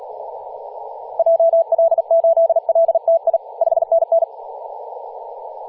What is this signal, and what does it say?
Morse code, heard as a keyed tone, about 37 wpm, 660 Hz, with radio noise. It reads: JL8LTI 5NN
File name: posts / 2016-aug-26-2236z-yj0cow-18-073mhz-cw